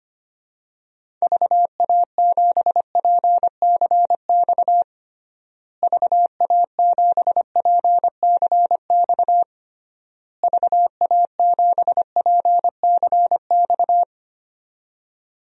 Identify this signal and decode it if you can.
Morse code, heard as a keyed tone, about 25 wpm, 700 Hz. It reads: VA7PCX VA7PCX VA7PCX
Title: DXCC-100 Call Signs - 6 Characters - Repeated 3x - 25wpm